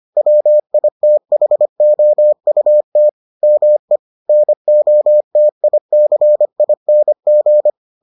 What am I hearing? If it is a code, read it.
Morse code, heard as a keyed tone, about 25 wpm, 600 Hz. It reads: WITHOUT ME NOTICING